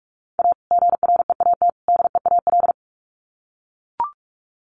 Morse code, heard as a keyed tone, about 44 wpm, 700 Hz. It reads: A GREAT DEAL
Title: Sets of 3 Words 45wpm